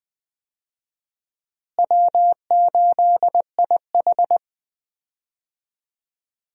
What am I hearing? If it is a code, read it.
Morse code, heard as a keyed tone, about 20 wpm, 700 Hz. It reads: W8IH
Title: USA Call Signs - 4 Characters - Repeated 2x - 20wpm